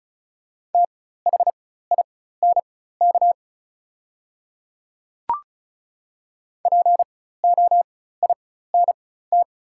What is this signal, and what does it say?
Morse code, heard as a keyed tone, about 34 wpm, spaced as for 16 wpm, 690 Hz. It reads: THINK POINT
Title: 4x Character Spacing - Top 200 Words - 35wpm